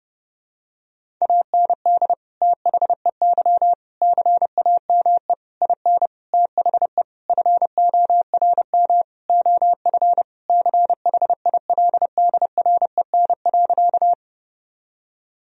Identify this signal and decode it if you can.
Morse code, heard as a keyed tone, about 30 wpm, 700 Hz. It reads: AND THEY CAME IN THE FORM OF CHILDREN.
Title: Sentences from Top 1000 Words - Review Part 1 - 30wpm